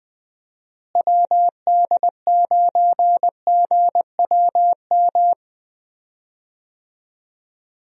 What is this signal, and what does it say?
Morse code, heard as a keyed tone, about 20 wpm, 700 Hz. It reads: WD9GWM
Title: USA Call Signs - 6 Characters - Repeated 2x - 20wpm